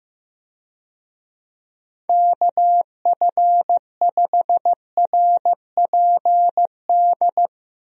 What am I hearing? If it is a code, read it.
Morse code, heard as a keyed tone, about 15 wpm, 700 Hz. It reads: KF5RPD
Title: USA Call Signs - 6 Characters - Repeated 3x - 15wpm